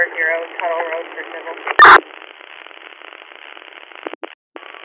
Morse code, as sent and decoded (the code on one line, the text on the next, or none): none